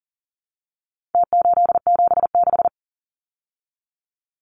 Morse code, heard as a keyed tone, about 40 wpm, 700 Hz. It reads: T876
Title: Numbers - Contest Style - 4-Digit 40wpm